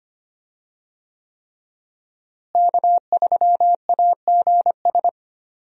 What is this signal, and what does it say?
Morse code, heard as a keyed tone, about 25 wpm, 700 Hz. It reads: K3AGS
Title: USA Call Signs - 5 Characters - Repeated 3x - 25wpm